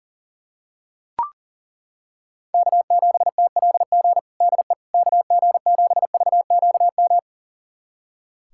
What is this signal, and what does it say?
Morse code, heard as a keyed tone, about 40 wpm, 700 Hz. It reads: K7TLG DE KG7VQM
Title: USA Callsign DE Callsign 40wpm